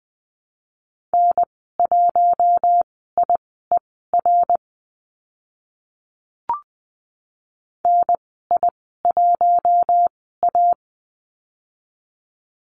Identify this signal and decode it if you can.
Morse code, heard as a keyed tone, about 20 wpm, spaced as for 14 wpm, 700 Hz. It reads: N1IER NI1A
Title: 055 Lesson 10g - Letter r - Call Signs 20wpm